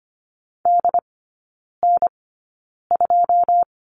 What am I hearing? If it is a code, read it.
Morse code, heard as a keyed tone, about 25 wpm, spaced as for 9 wpm, 690 Hz. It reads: DN2
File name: Lesson 14e - Number 2 - Three Characters 25wpm